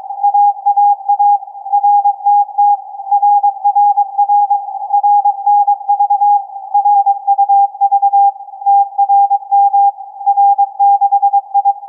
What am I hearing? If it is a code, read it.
Morse code, heard as a keyed tone, about 22 wpm, 810 Hz. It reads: AAA RTT RRR RNV RUV TRM R6I
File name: CAS4A-TURN-AIRSPY-01